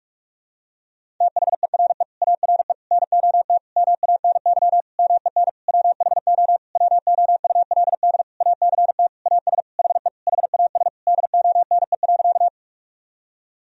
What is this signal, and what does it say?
Morse code, heard as a keyed tone, about 46 wpm, 700 Hz. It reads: THERE ARE NOT MANY MEN WHO WOULD ACT AS HE HAS DONE.
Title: Sentences from Top 1000 Words - Review Part 1 - 45wpm